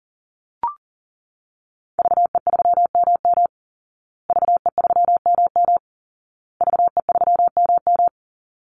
Morse code, heard as a keyed tone, about 40 wpm, 700 Hz. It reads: VE3MM VE3MM VE3MM
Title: Call Signs Repeated 2x - 40wpm